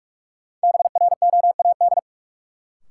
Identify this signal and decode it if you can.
Morse code, heard as a keyed tone, about 46 wpm, 700 Hz. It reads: BROAD
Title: Sentences from Top 1000 Words - New Word Drill - 45wpm